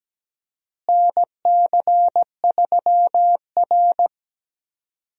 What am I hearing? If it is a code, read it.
Morse code, heard as a keyed tone, about 17 wpm, 700 Hz. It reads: NC3R
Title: USA Call Signs - 4 Characters - Repeated 2x - 17wpm